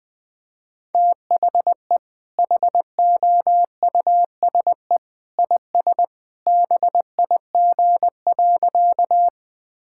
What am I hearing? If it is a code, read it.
Morse code, heard as a keyed tone, about 21 wpm, 700 Hz. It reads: THE HOUSE IS BIG.
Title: Sentences with 2-Word Phrases - 20wpm